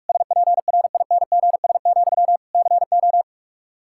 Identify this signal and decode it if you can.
Morse code, heard as a keyed tone, about 46 wpm, 700 Hz. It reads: SPRINGS, CO
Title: Practice QSO - Exchange 1 - Beginning - 45wpm